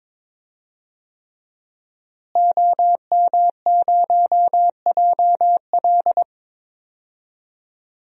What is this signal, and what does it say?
Morse code, heard as a keyed tone, about 22 wpm, 700 Hz. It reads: OM0JL
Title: DXCC-100 Call Signs - 5 Characters - Repeated 3x - 22wpm